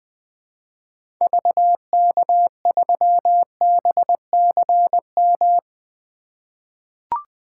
Morse code, heard as a keyed tone, about 20 wpm, 700 Hz. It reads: VK3BCM
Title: 2023 SOTA Activator Call Signs - 20wpm